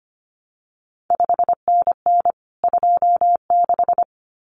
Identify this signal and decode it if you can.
Morse code, heard as a keyed tone, about 25 wpm, 700 Hz. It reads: 5NN 26
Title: CQ WW DX Contest Exchange Encore - 25wpm